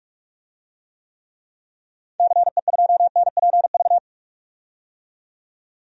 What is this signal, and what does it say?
Morse code, heard as a keyed tone, about 46 wpm, 700 Hz. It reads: CE2NPV
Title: DXCC-100 Call Signs - 6 Characters - 45wpm